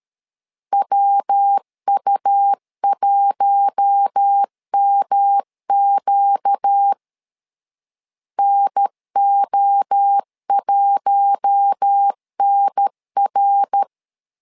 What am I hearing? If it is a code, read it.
Morse code, heard as a keyed tone, about 13 wpm, 800 Hz. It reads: WU1MQ NO1NR
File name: LEZIONI / 4 Q1 12wpm 2028